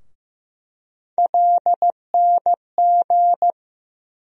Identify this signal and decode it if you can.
Morse code, heard as a keyed tone, about 15 wpm, 700 Hz. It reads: LNG
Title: USA Call Sign Postfixes - 3 Letters 15wpm